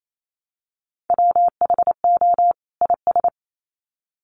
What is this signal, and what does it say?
Morse code, heard as a keyed tone, about 28 wpm, 700 Hz. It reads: WHO IS
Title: Sets of 2 Words 28wpm